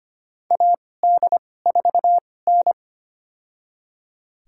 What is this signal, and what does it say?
Morse code, heard as a keyed tone, about 25 wpm, spaced as for 18 wpm, 700 Hz. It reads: AD4N
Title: USA Call Signs - 4 Characters - 2x Character Spacing - 25wpm